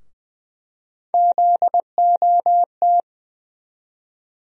Morse code, heard as a keyed tone, about 20 wpm, 700 Hz. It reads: ZOT
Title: USA Call Sign Postfixes - 3 Letters 20wpm